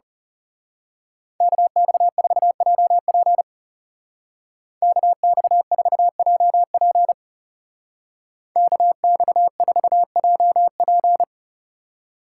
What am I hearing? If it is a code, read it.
Morse code, heard as a keyed tone, about 34 wpm, 700 Hz. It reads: KX4JP KX4JP KX4JP
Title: Speed Racing - USA Call Signs 30wpm